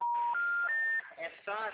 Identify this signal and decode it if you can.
Special information tone, with speech.
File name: out- -2017- - - .58298